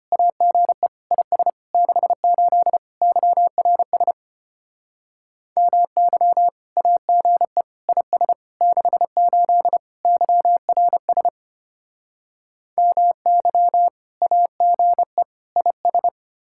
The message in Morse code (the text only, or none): AGE IS 68 YRS MY AGE IS 68 YRS MY AGE IS